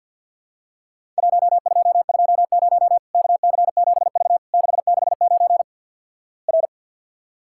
Morse code, heard as a keyed tone, about 52 wpm, 700 Hz. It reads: 1220 KC7V 669 R
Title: Worked All Europe DX Contest - QTC Exchange with Repeats - 50wpm